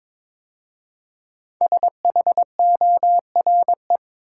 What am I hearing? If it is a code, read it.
Morse code, heard as a keyed tone, about 22 wpm, 700 Hz. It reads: SHORE
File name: 5 Letter Words - First 3 Letters Given - 22wpm